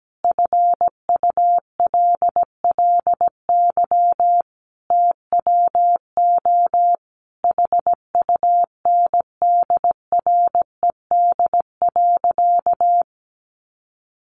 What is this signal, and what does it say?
Morse code, heard as a keyed tone, about 17 wpm, 700 Hz. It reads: FULLY TWO HUNDRED.
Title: Sentences from Top 1300 Words 17wpm